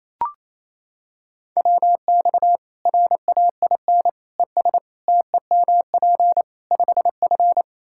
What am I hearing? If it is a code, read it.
Morse code, heard as a keyed tone, about 27 wpm, 700 Hz. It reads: WX RAIN ES TEMP 5F
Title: K1USN SST QSO Practice 28wpm